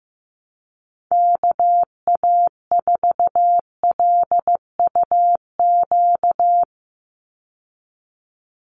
Morse code, heard as a keyed tone, about 15 wpm, 700 Hz. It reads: KA4LUQ
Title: USA Call Signs 15wpm